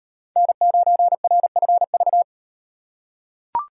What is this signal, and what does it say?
Morse code, heard as a keyed tone, about 36 wpm, 700 Hz. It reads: N9RFV